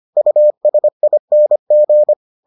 Morse code, heard as a keyed tone, about 25 wpm, 600 Hz. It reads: USING